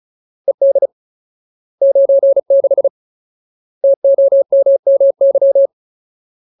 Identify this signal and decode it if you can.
Morse code, heard as a keyed tone, about 36 wpm, 550 Hz. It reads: ED 96 TOMMY
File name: Ultimate Word List / 35WPM Ultimate